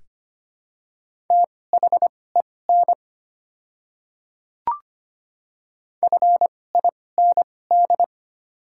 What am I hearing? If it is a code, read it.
Morse code, heard as a keyed tone, about 25 wpm, spaced as for 18 wpm, 700 Hz. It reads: THEN FIND
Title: 2x Character Spacing - Top 200 Words - 25wpm